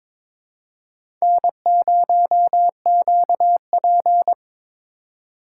Morse code, heard as a keyed tone, about 22 wpm, 700 Hz. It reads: N0QP
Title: USA Call Signs - 4 Characters - 3x Character Spacing - 22wpm